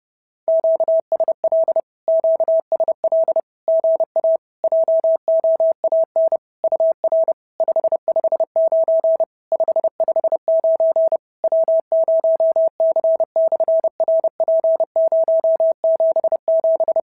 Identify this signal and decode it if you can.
Morse code, heard as a keyed tone, about 30 wpm, 650 Hz. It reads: QSL QSL GA JOAN UR 559 559 W0C/RP077
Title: SOTA QSO Practice 30wpm